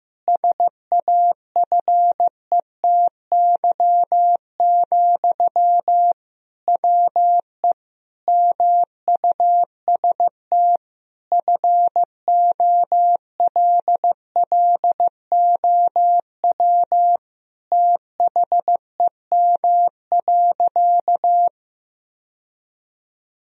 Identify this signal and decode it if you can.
Morse code, heard as a keyed tone, about 16 wpm, 700 Hz. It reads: SAFETY, WE MUST FOLLOW THEM.
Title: Sentences from Top 1000 Words - Review Part 4 - 15wpm